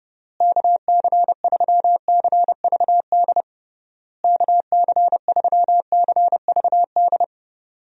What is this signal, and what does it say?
Morse code, heard as a keyed tone, about 30 wpm, 700 Hz. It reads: KC3CVD KC3CVD
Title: USA Call Signs - 6 Characters - Repeated 3x - 30wpm